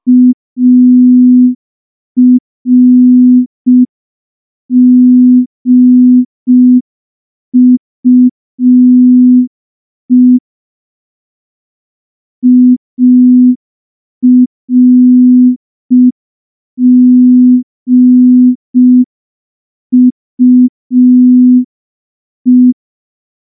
Morse code, heard as a keyed tone, about 5 wpm, 250 Hz. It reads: ARGUE ARGUE
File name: morse / morse-training-05WPM